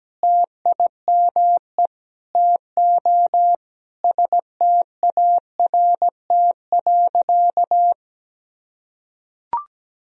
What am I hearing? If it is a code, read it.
Morse code, heard as a keyed tone, about 17 wpm, 700 Hz. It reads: TIME TO START.
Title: Sentences from Top 1000 Words - Review Part 1 - 17wpm